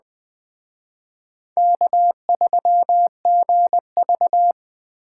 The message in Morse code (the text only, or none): K3GV